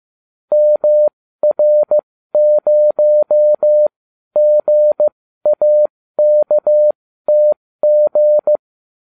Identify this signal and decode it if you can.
Morse code, heard as a keyed tone, about 15 wpm, spaced as for 12 wpm, 600 Hz. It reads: MR0GAKTG